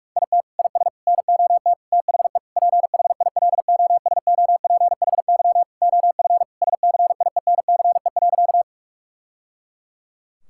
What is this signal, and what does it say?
Morse code, heard as a keyed tone, about 46 wpm, 700 Hz. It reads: IT IS NOT THE PHILOSOPHY OF SCIENCE.
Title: Sentences from Top 1000 Words - Review Part 6 - 45wpm